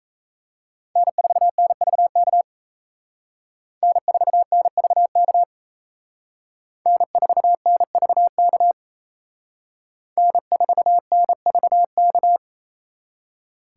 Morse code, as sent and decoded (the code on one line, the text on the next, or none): -. ....- -. ...- -.- / -. ....- -. ...- -.- / -. ....- -. ...- -.- / -. ....- -. ...- -.-
N4NVK N4NVK N4NVK N4NVK